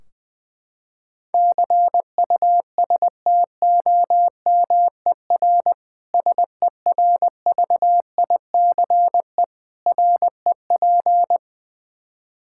Morse code, heard as a keyed tone, about 21 wpm, 700 Hz. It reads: CUSTOMER SERVICE REP
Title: Practice QSO - Exchange 3 - More Info - 20wpm